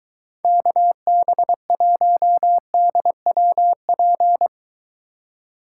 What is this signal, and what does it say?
Morse code, heard as a keyed tone, about 24 wpm, 700 Hz. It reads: KB1DWP